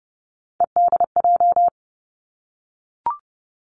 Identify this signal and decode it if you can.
Morse code, heard as a keyed tone, about 30 wpm, 700 Hz. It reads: EDJ
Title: Three Challenging Letter-Number Combinations 30wpm